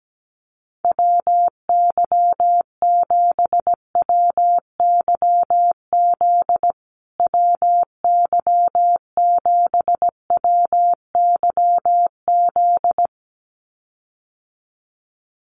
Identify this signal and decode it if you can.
Morse code, heard as a keyed tone, about 17 wpm, 700 Hz. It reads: WY7WYZ WY7WYZ
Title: USA Call Signs Repeated 2x 17wpm